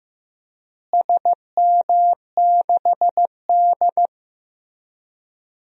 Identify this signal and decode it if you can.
Morse code, heard as a keyed tone, about 15 wpm, 700 Hz. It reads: SM6D
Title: DXCC-100 Call Signs - 4 Characters - Repeated 2x - 15wpm